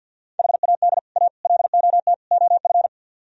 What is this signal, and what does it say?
Morse code, heard as a keyed tone, about 64 wpm, 700 Hz. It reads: HAD A LOT OF